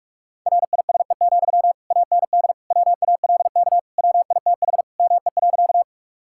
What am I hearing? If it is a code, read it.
Morse code, heard as a keyed tone, about 46 wpm, 700 Hz. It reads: RISE, AND WALK WITH ME.
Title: Sentences from Top 1000 Words - Review Part 6 - 45wpm